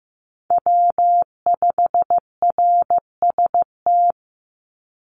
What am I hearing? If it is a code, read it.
Morse code, heard as a keyed tone, about 15 wpm, 700 Hz. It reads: W5RST
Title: NA SOTA Chaser Callsigns 15wpm